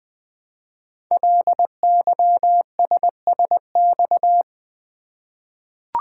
Morse code, heard as a keyed tone, about 20 wpm, 700 Hz. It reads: LYSSX
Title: Code Groups - 5 Letters - 20wpm